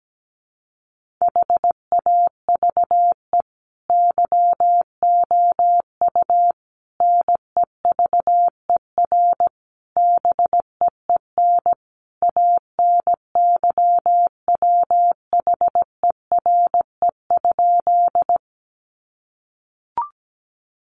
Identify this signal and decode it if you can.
Morse code, heard as a keyed tone, about 17 wpm, 700 Hz. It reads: HAVE YOU NEVER BEEN ANYWHERE?
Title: Sentences from Top 1400 Words 17wpm